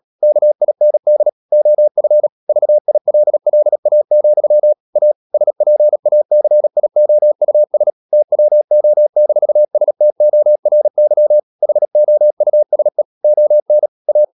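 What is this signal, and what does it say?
Morse code, heard as a keyed tone, about 37 wpm, 600 Hz. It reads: KIND OF VILLA, A SPACIOUS TWO-STORY HOUSE ON A
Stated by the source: Morse code practice at 35–39 wpm sending English text of Ken Follett's novel Never, from CW med Gnister.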